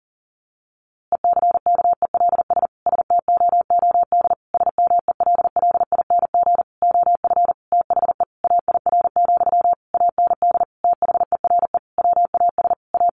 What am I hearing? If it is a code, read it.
Morse code, heard as a keyed tone, about 40 wpm, 700 Hz. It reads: ECKELS STOOD SMELLING OF THE AIR, AND THERE WAS A
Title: A Sound Of Thunder 40wpm